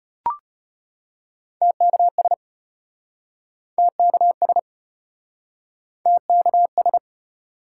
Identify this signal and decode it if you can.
Morse code, heard as a keyed tone, about 33 wpm, 700 Hz. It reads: TKS TKS TKS